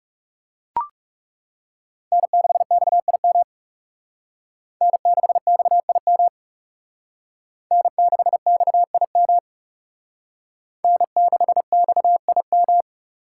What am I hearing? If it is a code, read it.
Morse code, heard as a keyed tone, about 37 wpm, 700 Hz. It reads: N6XIM N6XIM N6XIM N6XIM
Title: Speed Racing - USA 1x3 Call Signs 30wpm